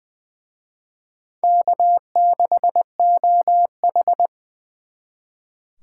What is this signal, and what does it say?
Morse code, heard as a keyed tone, about 24 wpm, 700 Hz. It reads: K6OH
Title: Speed Racing - USA 1x2 Call Signs 20wpm